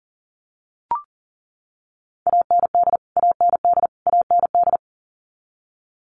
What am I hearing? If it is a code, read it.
Morse code, heard as a keyed tone, about 40 wpm, 700 Hz. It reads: AND AND AND
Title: Top 500 Words Repeated 40wpm